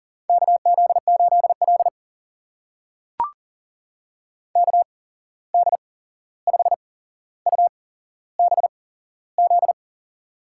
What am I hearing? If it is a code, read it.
Morse code, heard as a keyed tone, about 42 wpm, spaced as for 15 wpm, 690 Hz. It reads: KZ8L KD5UBZ
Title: USA Call Signs - 8x Character Spacing - 40wpm